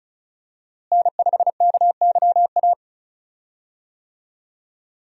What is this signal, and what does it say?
Morse code, heard as a keyed tone, about 34 wpm, 700 Hz. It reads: N5KYA
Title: USA Call Signs - 5 Characters - Repeated 2x - 35wpm